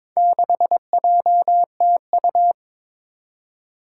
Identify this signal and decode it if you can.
Morse code, heard as a keyed tone, about 22 wpm, 700 Hz. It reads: 6JTU
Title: Four Challenging Letter-Number Combinations 22wpm